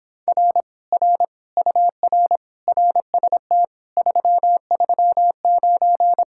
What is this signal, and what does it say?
Morse code, heard as a keyed tone, about 26 wpm, 700 Hz. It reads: R R UR RST 339